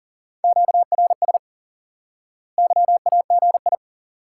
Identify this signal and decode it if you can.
Morse code, heard as a keyed tone, about 37 wpm, 700 Hz. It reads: QRS YAGI
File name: IQE Course 3 - Lesson 89 - YAGI - Two Elements - 40wpm